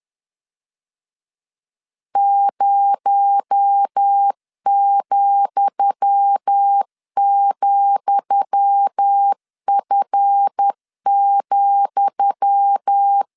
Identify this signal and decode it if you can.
Morse code, heard as a keyed tone, about 10 wpm, 800 Hz. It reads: 0,,F,
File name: LEZIONI / 10 SIGNS / 10D COMA 10wpm 3013